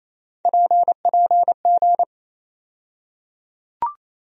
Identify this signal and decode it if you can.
Morse code, heard as a keyed tone, about 28 wpm, 700 Hz. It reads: PPG
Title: USA Call Sign Postfixes - 3 Letters 28wpm